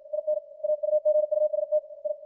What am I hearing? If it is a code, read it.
Morse code, heard as a keyed tone, about 56 wpm, 610 Hz. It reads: IN AUGUST I
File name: archived / CWBros